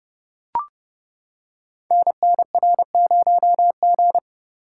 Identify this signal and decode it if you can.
Morse code, heard as a keyed tone, about 29 wpm, 700 Hz. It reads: NNR0G
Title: Code Groups - 5 Letter:Numbers - 30wpm